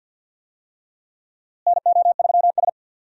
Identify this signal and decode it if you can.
Morse code, heard as a keyed tone, about 51 wpm, 700 Hz. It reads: NO3S
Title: USA Call Signs - 4 Characters - Repeated 3x - 50wpm